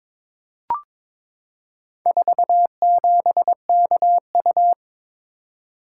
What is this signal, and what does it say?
Morse code, heard as a keyed tone, about 23 wpm, 700 Hz. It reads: 47KU
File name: practice / 22wpm / Four Letter-Number Combinations - 22wpm